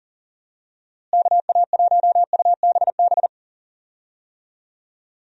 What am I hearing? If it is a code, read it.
Morse code, heard as a keyed tone, about 40 wpm, 700 Hz. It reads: KA1UBB
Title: USA Call Signs - 6 Characters - Repeated 2x - 40wpm